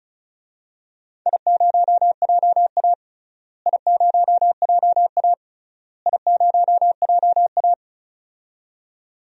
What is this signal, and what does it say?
Morse code, heard as a keyed tone, about 35 wpm, 700 Hz. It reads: I0JA I0JA I0JA
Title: DXCC-100 Call Signs - 4 Characters - Repeated 3x - 35wpm